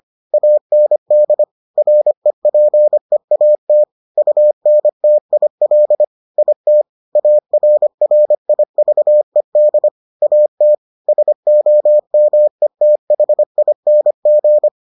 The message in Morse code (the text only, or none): AND REPEAT UNTIL IT ARRIVED AT SOMETHING